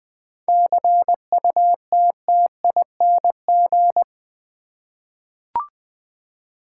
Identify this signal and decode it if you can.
Morse code, heard as a keyed tone, about 20 wpm, 700 Hz. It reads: CUTTING
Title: 7 Letter Words - First 3 Letters Given - 20wpm